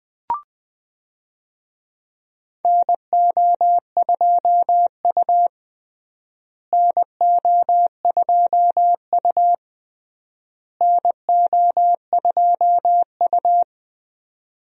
Morse code, heard as a keyed tone, about 20 wpm, 700 Hz. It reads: NO2U NO2U NO2U
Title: USA Call Signs - 4 Characters - Repeated 3x - 20wpm